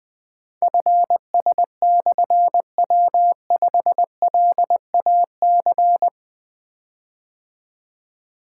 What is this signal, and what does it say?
Morse code, heard as a keyed tone, about 20 wpm, 700 Hz. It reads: FS/W5LAC
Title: Difficult Call Signs 20wpm